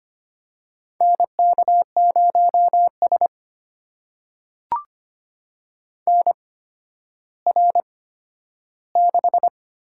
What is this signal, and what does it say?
Morse code, heard as a keyed tone, about 25 wpm, spaced as for 12 wpm, 700 Hz. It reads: NK0S NR6
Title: USA Call Signs - 4 Characters - 8x Character Spacing - 25wpm